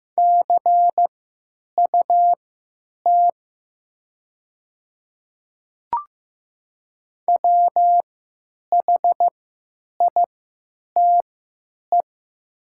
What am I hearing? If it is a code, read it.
Morse code, heard as a keyed tone, about 15 wpm, spaced as for 8 wpm, 700 Hz. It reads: CUT WHITE
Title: 3x Character Spacing - Top 300 Words - 15wpm